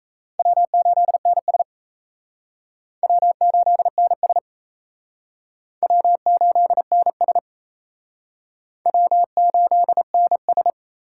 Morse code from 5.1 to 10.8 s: W8NS W8NS

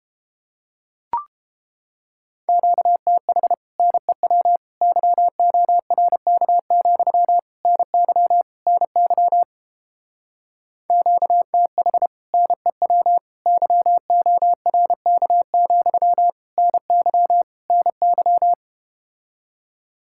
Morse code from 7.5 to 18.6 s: NY NY QTH NEW YORK, NY NY

